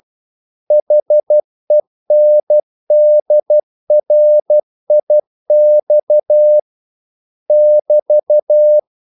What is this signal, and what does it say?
Morse code, heard as a keyed tone, about 12 wpm, 600 Hz. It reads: HENDRIX <BT>
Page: English quotations in CW (telegraphy - morse)